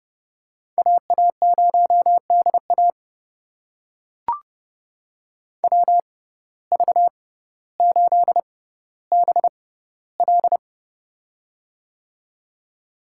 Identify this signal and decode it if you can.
Morse code, heard as a keyed tone, about 30 wpm, spaced as for 15 wpm, 700 Hz. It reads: AA0DA WV8BL
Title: USA Call Signs - 5 Characters - 6x Character Spacing - 30wpm